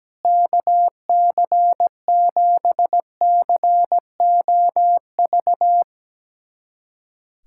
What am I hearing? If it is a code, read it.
Morse code, heard as a keyed tone, about 17 wpm, 700 Hz. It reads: KC7COV
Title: USA Callsign DE Callsign 17wpm